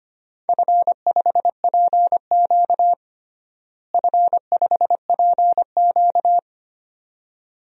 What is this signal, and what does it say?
Morse code, heard as a keyed tone, about 25 wpm, 700 Hz. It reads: F5PQ F5PQ
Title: DXCC-100 Call Signs - 4 Characters - Repeated 2x - 25wpm